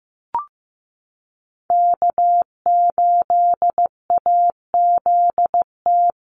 Morse code, heard as a keyed tone, about 15 wpm, 700 Hz. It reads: K8AZT
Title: Call Signs Repeated 2x - 15wpm